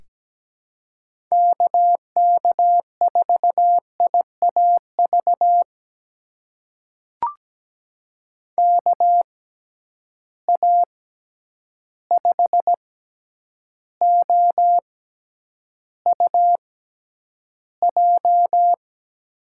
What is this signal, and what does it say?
Morse code, heard as a keyed tone, about 17 wpm, spaced as for 9 wpm, 700 Hz. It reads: KK4IAV KA5OUJ
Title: USA Call Signs - 6 Characters - 6x Character Spacing - 17wpm